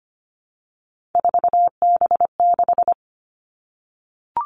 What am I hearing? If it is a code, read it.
Morse code, heard as a keyed tone, about 25 wpm, 700 Hz. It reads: 4B6
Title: Three Challenging Letter-Number Combinations 25wpm